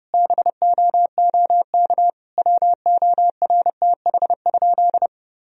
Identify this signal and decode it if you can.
Morse code, heard as a keyed tone, about 30 wpm, 700 Hz. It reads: BOOK WORTH?
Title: Sentences from Top 1000 Words - Review Part 1 - 30wpm